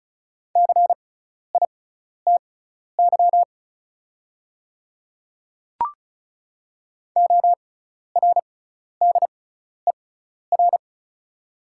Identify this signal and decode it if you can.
Morse code, heard as a keyed tone, about 36 wpm, spaced as for 12 wpm, 690 Hz. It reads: CITY ORDER
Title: 6x Character Spacing - Top 500 Words - 35wpm